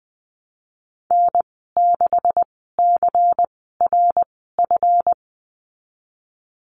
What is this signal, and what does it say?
Morse code, heard as a keyed tone, about 20 wpm, spaced as for 15 wpm, 700 Hz. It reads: N6CRF
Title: 167 Lesson 27g - Number 7 - Call Signs 20wpm